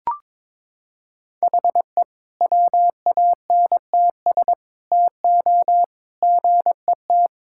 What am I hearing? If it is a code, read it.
Morse code, heard as a keyed tone, about 22 wpm, 700 Hz. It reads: HE WANTS TO GET
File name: practice / 22wpm / Sentences with 4-Word Phrases - 22wpm